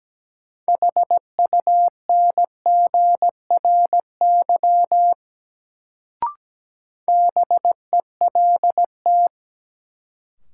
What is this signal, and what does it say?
Morse code, heard as a keyed tone, about 17 wpm, 700 Hz. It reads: HUNGRY BELT
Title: Sentences from Top 1400 Words - New Word Drill Alt - 17wpm